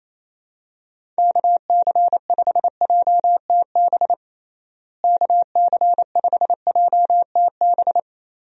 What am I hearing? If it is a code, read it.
Morse code, heard as a keyed tone, about 28 wpm, 700 Hz. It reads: KC5JTB KC5JTB
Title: USA Call Signs - 6 Characters - Repeated 3x - 28wpm